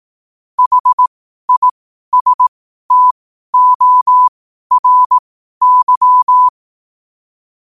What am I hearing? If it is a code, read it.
Morse code, heard as a keyed tone, about 18 wpm, spaced as for 12 wpm, 1000 Hz. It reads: HISTORY
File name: 12wpm